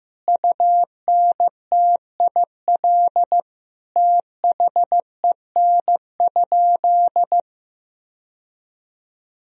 Morse code, heard as a keyed tone, about 16 wpm, 700 Hz. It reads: UNTIL THEN?
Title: Sentences from Top 1000 Words - Review Part 1 - 15wpm